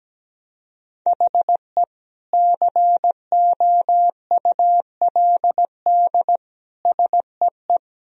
Morse code, heard as a keyed tone, about 17 wpm, 700 Hz. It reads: HE COULD SEE
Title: Sentences from Top 1000 Words - Review Part 1 - 17wpm